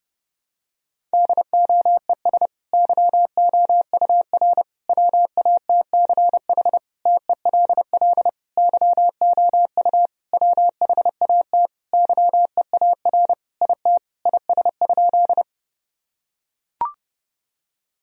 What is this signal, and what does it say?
Morse code, heard as a keyed tone, about 30 wpm, 700 Hz. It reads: DOES YOUR WATCH TELL YOU WHAT YEAR IT IS?
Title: Sentences from Top 1000 Words - Review Part 1 - 30wpm